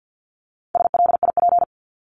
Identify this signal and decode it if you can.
Morse code, heard as a keyed tone, about 50 wpm, 700 Hz. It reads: SLIP
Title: Top 1751-2000 Words 50wpm